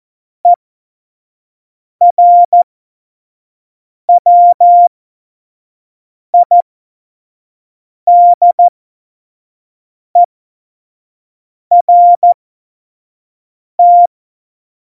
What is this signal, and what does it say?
Morse code, heard as a keyed tone, about 14 wpm, spaced as for 5 wpm, 700 Hz. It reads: ERWIDERT
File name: Morsekurs / Witze / Witz 08. September 2020, 25 F70